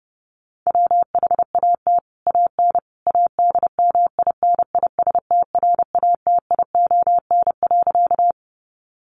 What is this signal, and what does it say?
Morse code, heard as a keyed tone, about 30 wpm, 700 Hz. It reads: WHAT AN ADMINISTRATION.
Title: Sentences from Top 1300 Words 30wpm